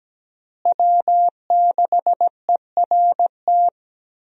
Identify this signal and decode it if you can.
Morse code, heard as a keyed tone, about 17 wpm, 700 Hz. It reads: W6ERT
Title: USA Call Signs - 5 Characters - Repeated 2x - 17wpm